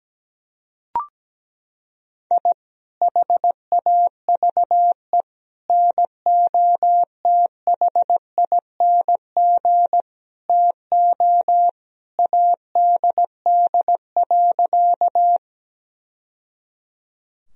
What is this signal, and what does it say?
Morse code, heard as a keyed tone, about 17 wpm, 700 Hz. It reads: I HAVE NOTHING TO ADD.
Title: Sentences from Top 1000 Words - Review Part 1 - 17wpm